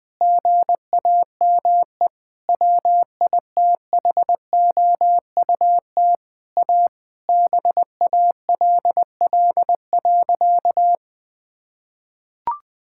Morse code, heard as a keyed tone, about 20 wpm, 700 Hz. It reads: GAME WITHOUT A BALL.
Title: Sentences from Top 1000 Words - Review Part 6 - 20wpm